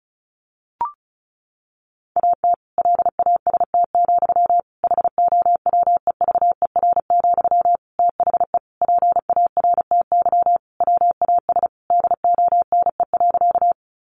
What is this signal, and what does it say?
Morse code, heard as a keyed tone, about 34 wpm, 700 Hz. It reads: AT LAST, HOWEVER, THE PARTY WAS DONE.
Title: Sentences from Top 1400 Words 35wpm